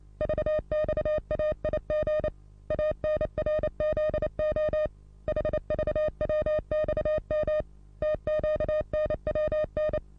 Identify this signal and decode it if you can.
Morse code, heard as a keyed tone, about 28 wpm, 610 Hz. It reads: VXAIG ANRZO HVWXM TQNWN
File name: repos / Morsekurs / DJ4UF Morsen27